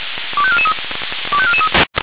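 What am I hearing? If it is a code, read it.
Five-tone selective call (ZVEI), heard as quick runs of five tones, tones of about 70 ms. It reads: 25443 25443
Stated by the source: Received on an SDR.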